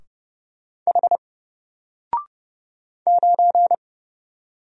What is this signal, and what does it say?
Morse code, heard as a keyed tone, about 29 wpm, 700 Hz. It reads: H 9
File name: Single-Letter-Number-30wpm